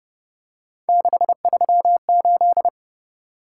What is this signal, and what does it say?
Morse code, heard as a keyed tone, about 30 wpm, 700 Hz. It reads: 638
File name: Speed-Racing - Three Letter-Number Combinations - 25wpm